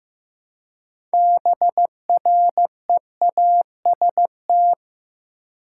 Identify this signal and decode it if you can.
Morse code, heard as a keyed tone, about 15 wpm, 700 Hz. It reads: BREAST
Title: Six Letter Words - 15wpm